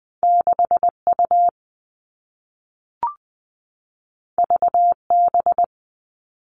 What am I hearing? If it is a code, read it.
Morse code, heard as a keyed tone, about 20 wpm, 700 Hz. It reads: 6 U V B
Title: All Cut Numbers - 2-Digit 20wpm